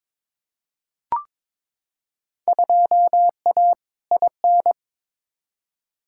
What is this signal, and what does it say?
Morse code, heard as a keyed tone, about 22 wpm, 700 Hz. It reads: 2A IN
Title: Field Day Exchanges - 22wpm